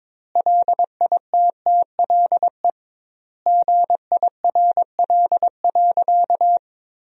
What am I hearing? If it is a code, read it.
Morse code, heard as a keyed tone, about 22 wpm, 700 Hz. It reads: LITTLE GIRL.
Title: 2x Word-Spacing - Sentences from Top 400 Words - 22wpm